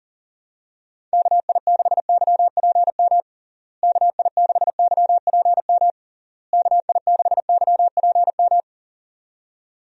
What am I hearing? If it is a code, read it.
Morse code, heard as a keyed tone, about 40 wpm, 700 Hz. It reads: KI6YPM KI6YPM KI6YPM
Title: USA Call Signs - 6 Characters - Repeated 3x - 40wpm